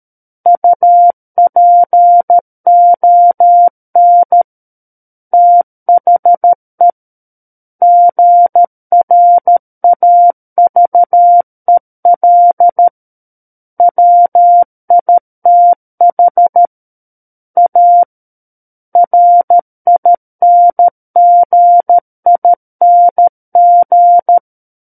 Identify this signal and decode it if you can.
Morse code, heard as a keyed tone, about 13 wpm, 700 Hz. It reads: UPON THE GRAVEL WITH A RINGING
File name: Book One / 04-Chapter 4 - 13 WPM